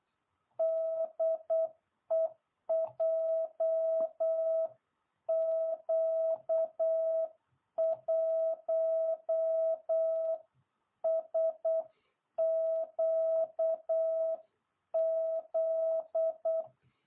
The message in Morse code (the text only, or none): DE JQ1SQZ